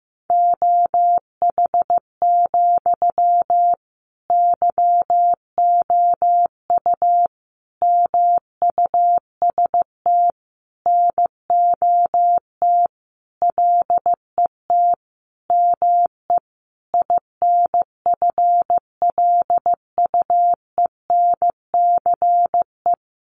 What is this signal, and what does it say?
Morse code, heard as a keyed tone, about 15 wpm, 700 Hz. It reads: OH, YOU MUST NOT LET ME INFLUENCE
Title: Sentences from Top 1100 Words 15wpm